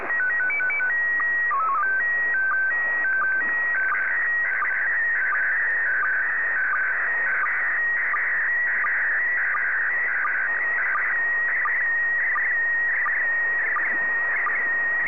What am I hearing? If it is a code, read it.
SSTV (slow-scan television), PD 90.
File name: sstv / PD90